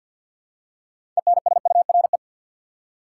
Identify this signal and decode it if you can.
Morse code, heard as a keyed tone, about 53 wpm, 700 Hz. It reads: ENSURE
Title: Six Letter Words - 50wpm